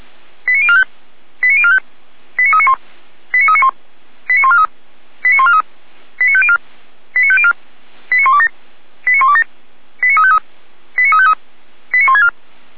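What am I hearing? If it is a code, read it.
Five-tone selective call (ZVEI), heard as quick runs of five tones, tones of about 70 ms. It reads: 89045 89045 89391 89391 89143 89143 89504 89504 89127 89127 89353 89353 89164
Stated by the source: ZVEI five-tone selective call received by SDR.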